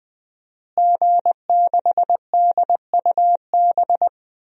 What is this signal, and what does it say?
Morse code, heard as a keyed tone, about 20 wpm, 700 Hz. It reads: G6DUB
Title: DXCC-100 Call Signs - 5 Characters - Repeated 2x - 20wpm